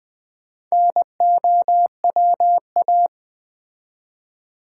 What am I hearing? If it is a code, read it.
Morse code, heard as a keyed tone, about 20 wpm, 700 Hz. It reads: NOWA
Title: Build A Word - 4 Letters At A Time - 8-Letter Words - 20wpm